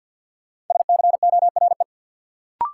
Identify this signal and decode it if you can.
Morse code, heard as a keyed tone, about 53 wpm, 700 Hz. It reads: SCORE
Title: Sentences from Top 1600 Words - New Word Drill Alt - 50wpm